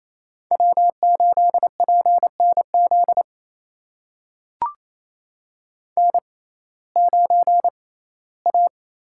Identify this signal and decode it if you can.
Morse code, heard as a keyed tone, about 28 wpm, spaced as for 17 wpm, 700 Hz. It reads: W8PNZ N9A
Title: USA Call Signs - 5 Characters - 6x Character Spacing - 28wpm